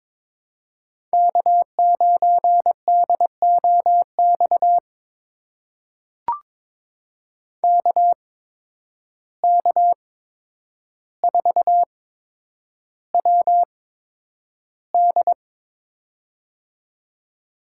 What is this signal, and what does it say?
Morse code, heard as a keyed tone, about 21 wpm, spaced as for 9 wpm, 700 Hz. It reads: K9DOX KK4WD
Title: USA Call Signs - 5 Characters - 8x Character Spacing - 22wpm